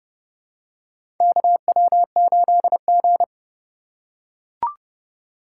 Morse code, heard as a keyed tone, about 30 wpm, 700 Hz. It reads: KW8G
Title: USA Call Signs - 4 Characters - 3x Character Spacing - 30wpm